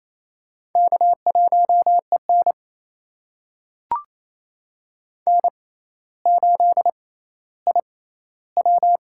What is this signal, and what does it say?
Morse code, heard as a keyed tone, about 28 wpm, spaced as for 14 wpm, 700 Hz. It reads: K1EN N8IW
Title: USA Call Signs - 4 Characters - 6x Character Spacing - 28wpm